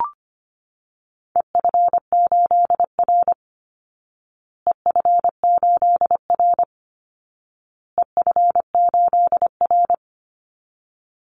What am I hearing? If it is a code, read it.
Morse code, heard as a keyed tone, about 25 wpm, 700 Hz. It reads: EF8R EF8R EF8R
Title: Call Signs Repeated 2x - 25wpm